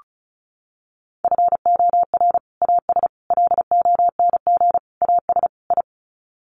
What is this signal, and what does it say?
Morse code, heard as a keyed tone, about 34 wpm, 700 Hz. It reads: FOR AS LONG AS I
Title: Sets of 5 Words Encore2 35wpm